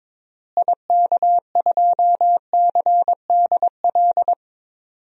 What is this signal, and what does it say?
Morse code, heard as a keyed tone, about 22 wpm, 700 Hz. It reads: IK2CDL
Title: DXCC-100 Call Signs - 6 Characters - Repeated 3x - 22wpm